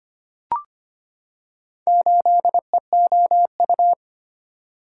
Morse code, heard as a keyed tone, about 25 wpm, 700 Hz. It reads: 8EOU
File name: practice / 25wpm / Four Letter-Number Combinations - 25wpm